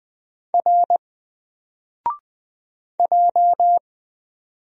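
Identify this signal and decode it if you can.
Morse code, heard as a keyed tone, about 20 wpm, 700 Hz. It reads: R J
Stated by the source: Single letter and number Morse drill at 20 wpm, from N4LCD.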